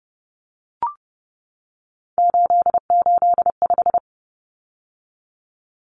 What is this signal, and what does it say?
Morse code, heard as a keyed tone, about 30 wpm, 700 Hz. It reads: 885
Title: Numbers - 3-Digit 30wpm